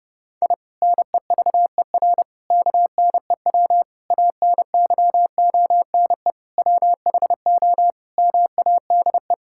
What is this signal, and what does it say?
Morse code, heard as a keyed tone, about 30 wpm, 700 Hz. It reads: I NEVER KNEW ANYONE WHO MADE
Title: Sentences from Top 1000 Words - Review Part 6 - 30wpm